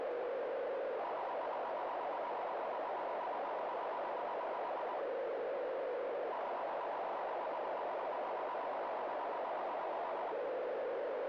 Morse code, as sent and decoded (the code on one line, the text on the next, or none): none